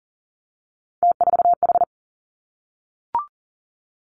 Morse code, heard as a keyed tone, about 41 wpm, 700 Hz. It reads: T4H
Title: Three Challenging Letter-Number Combinations 40wpm